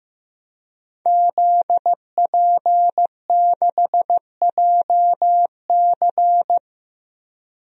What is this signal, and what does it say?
Morse code, heard as a keyed tone, about 15 wpm, 700 Hz. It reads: ZP6JC
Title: DXCC-100 Call Signs - 5 Characters - 15wpm